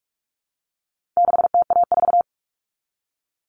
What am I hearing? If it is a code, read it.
Morse code, heard as a keyed tone, about 44 wpm, 700 Hz. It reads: 6TA4
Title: All Cut Numbers - 4-Digit 45wpm